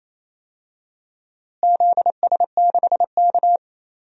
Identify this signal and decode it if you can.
Morse code, heard as a keyed tone, about 28 wpm, 700 Hz. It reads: ZS6K
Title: DXCC-100 Call Signs - 4 Characters - Repeated 2x - 28wpm